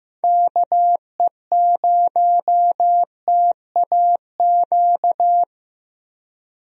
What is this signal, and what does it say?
Morse code, heard as a keyed tone, about 15 wpm, 700 Hz. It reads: KE0TAQ
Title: USA Call Signs - 6 Characters - Repeated 2x - 15wpm